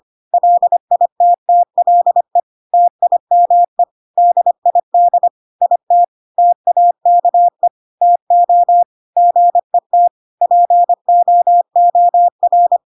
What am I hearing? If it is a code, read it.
Morse code, heard as a keyed tone, about 25 wpm, 700 Hz. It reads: LITTLE TIME DID IT TAKE TO GET POOR
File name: Pinocchio / Pinocchio ch4 25 25 ik1hge